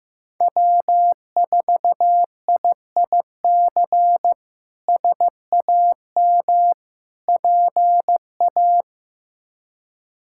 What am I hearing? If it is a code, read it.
Morse code, heard as a keyed tone, about 15 wpm, 700 Hz. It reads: W4IIC SAM PA
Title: K1USN SST QSO Practice - 15wpm